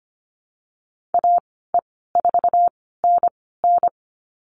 Morse code, heard as a keyed tone, about 25 wpm, spaced as for 16 wpm, 690 Hz. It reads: AE4NN
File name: Lesson 15g - Number 5 - Call Signs 25wpm